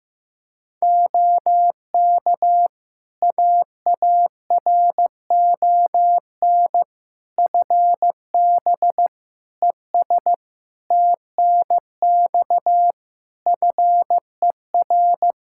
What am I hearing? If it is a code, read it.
Morse code, heard as a keyed tone, about 15 wpm, 700 Hz. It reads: OK AARON FB ES TNX FER
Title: Practice QSO - Exchange 2 - Info - 15wpm